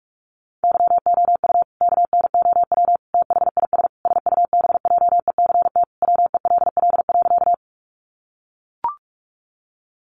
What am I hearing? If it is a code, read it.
Morse code, heard as a keyed tone, about 44 wpm, 700 Hz. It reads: YOU KNOW THIS SUBJECT WELL.
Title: Sentences from Top 800 Words 45wpm